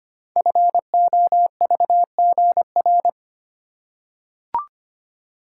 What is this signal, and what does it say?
Morse code, heard as a keyed tone, about 25 wpm, 700 Hz. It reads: FOVGR
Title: Code Groups - 5 Letters - 25wpm